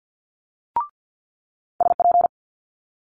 Morse code, heard as a keyed tone, about 50 wpm, 700 Hz. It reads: SP
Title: Multi-letter Phonemes Quick Fire 50wpm